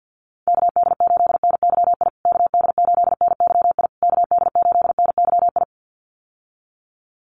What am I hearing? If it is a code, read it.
Morse code, heard as a keyed tone, about 48 wpm, 700 Hz. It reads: KD8NYI KD8NYI KD8NYI
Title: USA Call Signs Repeated 3x 50wpm